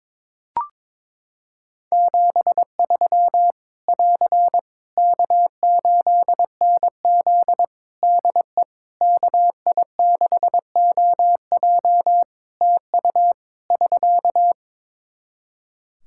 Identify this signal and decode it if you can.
Morse code, heard as a keyed tone, about 22 wpm, 700 Hz. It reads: 73 <AR> K8NZ DE KI6OJ TU <SK>
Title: Practice QSO - Exchange 4 - Ending - 22wpm